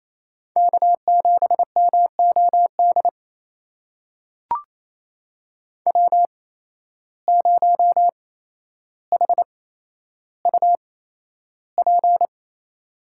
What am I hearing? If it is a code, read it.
Morse code, heard as a keyed tone, about 28 wpm, spaced as for 12 wpm, 700 Hz. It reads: K7MOD W0HUP
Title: USA Call Signs - 5 Characters - 8x Character Spacing - 28wpm